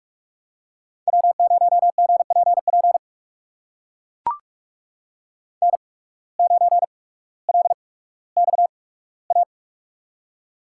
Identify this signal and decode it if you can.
Morse code, heard as a keyed tone, about 44 wpm, spaced as for 19 wpm, 700 Hz. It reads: W0GPP N9LXA
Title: USA Call Signs - 5 Characters - 8x Character Spacing - 45wpm